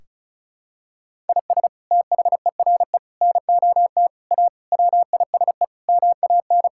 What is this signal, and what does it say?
Morse code, heard as a keyed tone, about 36 wpm, 700 Hz. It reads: IS THERE NOT A WISE MAN
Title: Sentences from Top 1600 Words 35wpm